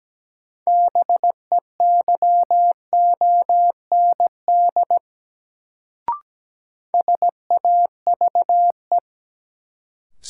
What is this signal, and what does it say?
Morse code, heard as a keyed tone, about 17 wpm, 700 Hz. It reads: BEYOND SAVE
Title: Sentences from Top 1100 Words - New Word Drill Alt - 17wpm